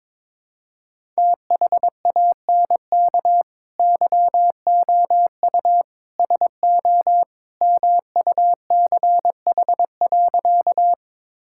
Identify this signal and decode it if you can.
Morse code, heard as a keyed tone, about 21 wpm, 700 Hz. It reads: THANK YOU SO MUCH.
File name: practice / 22wpm / Sentences with 2-Word Phrases - 22wpm